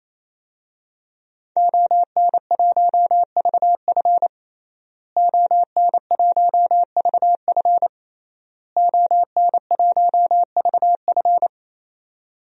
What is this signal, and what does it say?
Morse code, heard as a keyed tone, about 28 wpm, 700 Hz. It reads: ON1VF ON1VF ON1VF
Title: DXCC-100 Call Signs - 5 Characters - Repeated 3x - 28wpm